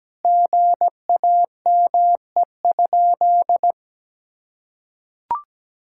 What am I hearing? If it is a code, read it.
Morse code, heard as a keyed tone, about 17 wpm, 700 Hz. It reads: GAME?
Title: Sentences from Top 1000 Words - Review Part 5 - 17wpm